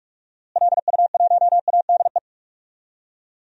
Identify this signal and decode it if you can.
Morse code, heard as a keyed tone, about 44 wpm, 700 Hz. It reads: LU1ADE
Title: DXCC-100 Call Signs - 6 Characters - Repeated 2x - 45wpm